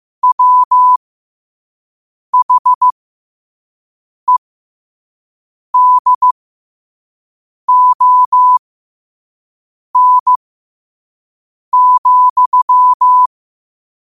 Morse code, heard as a keyed tone, about 15 wpm, spaced as for 5 wpm, 1000 Hz. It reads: WHEDON,